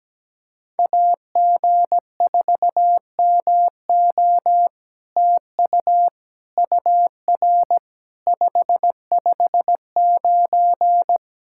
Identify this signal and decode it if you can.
Morse code, heard as a keyed tone, about 17 wpm, 700 Hz. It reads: AG4MO TU UR 559
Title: SOTA QSO Practice 17wpm